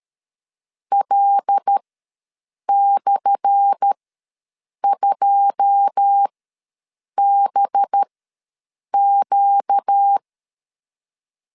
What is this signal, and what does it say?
Morse code, heard as a keyed tone, about 13 wpm, spaced as for 7 wpm, 800 Hz. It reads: L/2BQ
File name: LEZIONI / 10 SIGNS / 10A SLASH ws 12wpm 3443